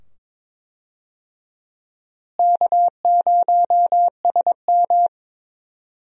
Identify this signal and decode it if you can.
Morse code, heard as a keyed tone, about 22 wpm, 700 Hz. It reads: K0SM
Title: USA Call Signs - 4 Characters - Repeated 2x - 22wpm